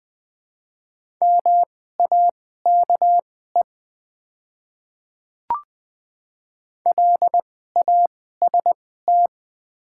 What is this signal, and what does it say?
Morse code, heard as a keyed tone, about 20 wpm, spaced as for 15 wpm, 700 Hz. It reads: MAKE LAST
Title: 2x Character Spacing - Top 300 Words - 20wpm